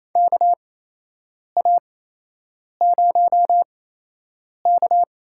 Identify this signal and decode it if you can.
Morse code, heard as a keyed tone, about 28 wpm, spaced as for 8 wpm, 700 Hz. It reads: KA0K
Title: USA Call Signs - 8x Character Spacing - 28wpm